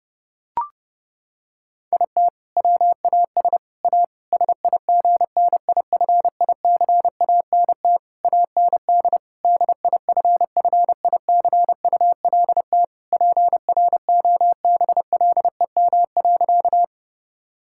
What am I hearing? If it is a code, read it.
Morse code, heard as a keyed tone, about 32 wpm, 700 Hz. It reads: IT WAS A SIGNIFICANT AND DIFFICULT PROBLEM.
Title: Sentences from Top 1000 Words - Review Part 6 - 30wpm